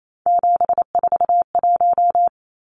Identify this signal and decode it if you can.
Morse code, heard as a keyed tone, about 28 wpm, 700 Hz. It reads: 741
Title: Numbers - 3-Digit 28wpm